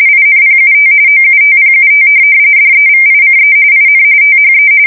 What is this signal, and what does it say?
RTTY, about 45.45 baud. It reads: INPUT OUTPUT DEVICE IN THE
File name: fsk170 / teletype-corp